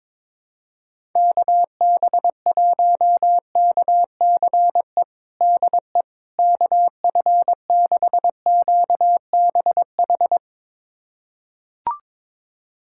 Morse code, heard as a keyed tone, about 22 wpm, 700 Hz. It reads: KB1KCE DE KF6QBH
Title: USA Callsign DE Callsign 22wpm